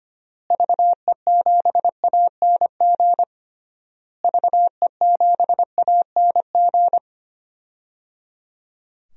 Morse code, heard as a keyed tone, about 25 wpm, 700 Hz. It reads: VE7ANG VE7ANG
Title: DXCC-100 Call Signs - 6 Characters - Repeated 3x - 25wpm